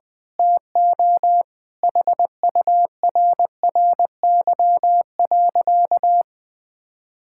Morse code, heard as a keyed tone, about 20 wpm, 700 Hz. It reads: TO HURRY.
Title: Sentences with 5-Word Phrases - 20wpm